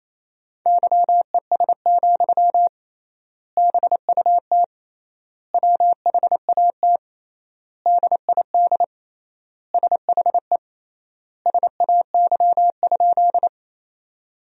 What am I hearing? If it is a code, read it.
Morse code, heard as a keyed tone, about 28 wpm, 700 Hz. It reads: YES, BUT WHAT DID SHE SAY?
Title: 3x Word-Spacing - Sentences from Top 500 Words - 28wpm